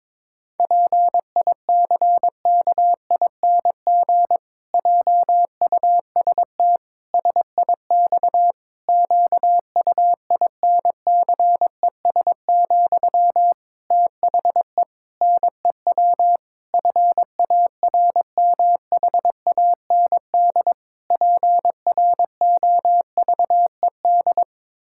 Morse code, heard as a keyed tone, about 21 wpm, 700 Hz. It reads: PICKING JUST SIX QUINCES, THE NEW FARMHAND PROVED
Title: Pangram Challenge v2 - 22wpm